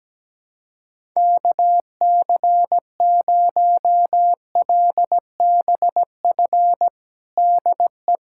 Morse code, heard as a keyed tone, about 17 wpm, 700 Hz. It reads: KC0LBF DE
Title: USA Callsign DE Callsign - 6 Characters - 17wpm